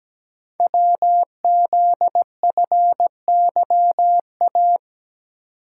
Morse code, heard as a keyed tone, about 17 wpm, 700 Hz. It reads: WZFYA
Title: Code Groups - 5 Letters - 17wpm